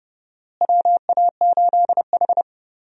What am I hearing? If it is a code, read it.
Morse code, heard as a keyed tone, about 30 wpm, 700 Hz. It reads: WA8H
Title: USA Call Signs - 4 Characters - Repeated 3x - 30wpm